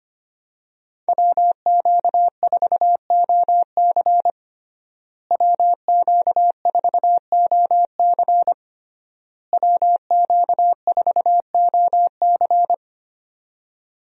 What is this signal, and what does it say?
Morse code, heard as a keyed tone, about 25 wpm, 700 Hz. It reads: WQ4OC WQ4OC WQ4OC
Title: USA Call Signs - 5 Characters - Repeated 3x - 25wpm